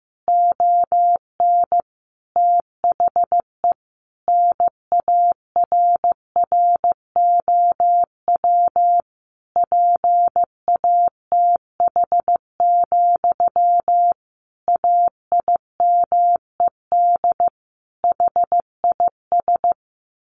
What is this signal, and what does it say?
Morse code, heard as a keyed tone, about 14 wpm, 700 Hz. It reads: ON THE NARROW PATH, AIMED HIS
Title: A Sound Of Thunder 15wpm